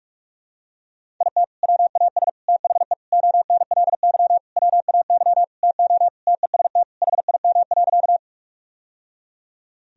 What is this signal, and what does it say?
Morse code, heard as a keyed tone, about 41 wpm, 700 Hz. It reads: IT WAS THE ONLY WAY TO TEST HIM.
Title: Sentences from Top 1000 Words - Review Part 3 - 45wpm